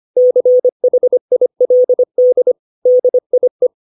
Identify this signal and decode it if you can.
Morse code, heard as a keyed tone, about 25 wpm, 500 Hz. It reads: CHILD DIE
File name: 632 words / 632 Words 25WPM